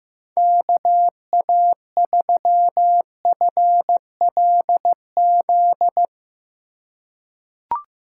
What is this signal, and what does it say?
Morse code, heard as a keyed tone, about 15 wpm, 700 Hz. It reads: KA3FLZ
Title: USA Call Signs - 4x Character Spacing - 15wpm